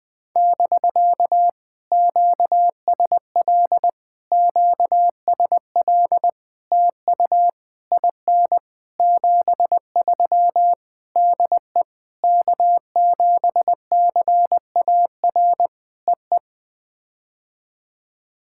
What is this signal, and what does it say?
Morse code, heard as a keyed tone, about 19 wpm, 700 Hz. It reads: <BK> QSL QSL TU IN 73 DE K7CAR EE
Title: POTA QSO Practice - 20wpm